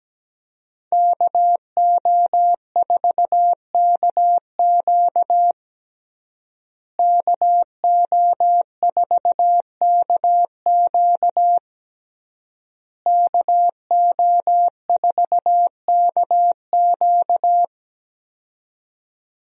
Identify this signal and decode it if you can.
Morse code, heard as a keyed tone, about 17 wpm, 700 Hz. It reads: KO4KQ KO4KQ KO4KQ
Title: USA Call Signs - 5 Characters - Repeated 3x - 17wpm